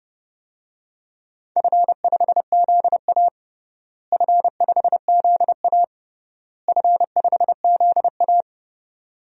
Morse code, heard as a keyed tone, about 30 wpm, 700 Hz. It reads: F5ZA F5ZA F5ZA
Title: DXCC-100 Call Signs - 4 Characters - Repeated 3x - 30wpm